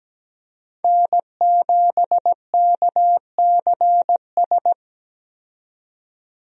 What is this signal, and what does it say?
Morse code, heard as a keyed tone, about 17 wpm, 700 Hz. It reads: N7KCS
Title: USA Call Signs - 5 Characters - 17wpm